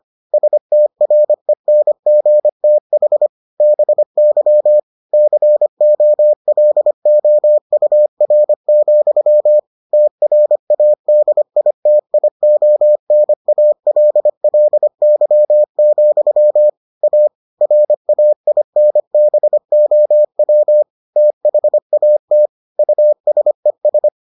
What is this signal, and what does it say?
Morse code, heard as a keyed tone, about 25 wpm, 600 Hz. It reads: STRENGTH BY COLOUR, TRADITIONALLY, A RAINBOW THAT USES